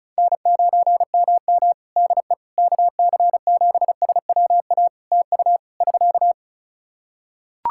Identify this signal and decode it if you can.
Morse code, heard as a keyed tone, about 34 wpm, 700 Hz. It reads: N9MM DE KC7SWA TU <SK>
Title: Practice QSO - Exchange 4 - Ending - 35wpm